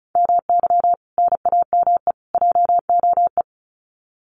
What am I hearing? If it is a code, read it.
Morse code, heard as a keyed tone, about 34 wpm, 700 Hz. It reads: MY NAME JOE
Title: Common QSO Phrases 35wpm